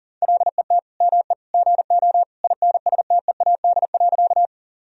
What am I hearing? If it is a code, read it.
Morse code, heard as a keyed tone, about 37 wpm, 700 Hz. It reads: LET ME GO INSTEAD.
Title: Sentences from Top 1000 Words - Review Part 4 - 40wpm